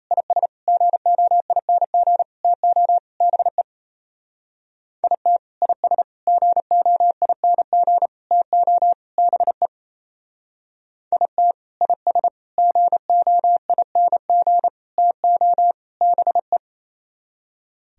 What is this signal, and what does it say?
Morse code, heard as a keyed tone, about 34 wpm, 700 Hz. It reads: IS GOING TO BE IT IS GOING TO BE IT IS GOING TO BE
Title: Speed-Racing - Sets of 5 Words 28wpm